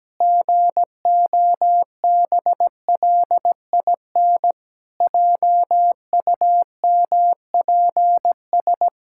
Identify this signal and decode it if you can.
Morse code, heard as a keyed tone, about 17 wpm, 700 Hz. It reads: GOBLIN JUMPS
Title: Pangram Challenge v2 - 17wpm